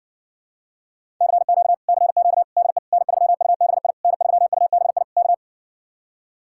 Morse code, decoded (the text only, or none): CQ CQ DE N3UBI N3UBI K